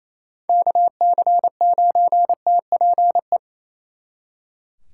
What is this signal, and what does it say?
Morse code, heard as a keyed tone, about 36 wpm, 700 Hz. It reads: KC9TPE